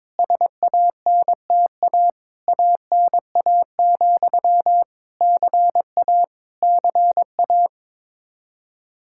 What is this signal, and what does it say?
Morse code, heard as a keyed tone, about 21 wpm, 700 Hz. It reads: SANTA ANA, CA CA
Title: Practice QSO - Exchange 1 - Beginning - 22wpm